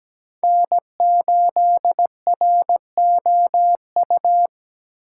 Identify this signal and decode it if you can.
Morse code, heard as a keyed tone, about 20 wpm, 700 Hz. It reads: N8ROU
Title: Speed Racing - USA 1x3 Call Signs 17wpm